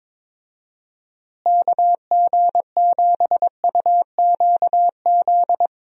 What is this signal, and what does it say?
Morse code, heard as a keyed tone, about 22 wpm, 700 Hz. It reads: KG7UQZ
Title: USA Call Signs - 6 Characters - Repeated 2x - 22wpm